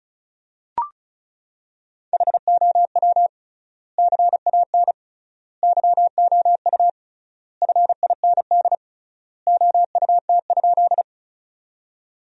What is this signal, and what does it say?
Morse code, heard as a keyed tone, about 36 wpm, 700 Hz. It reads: HOW CAN YOU FIND OUT?
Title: 3x Word-Spacing - Sentences from Top 200 Words - 35wpm